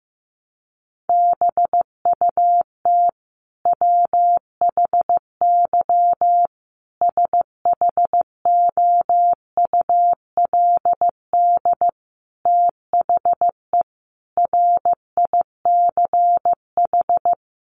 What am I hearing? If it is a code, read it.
Morse code, heard as a keyed tone, about 15 wpm, 700 Hz. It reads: BUT WHY SHOULD THE RICH
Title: Sentences from Top 1100 Words 15wpm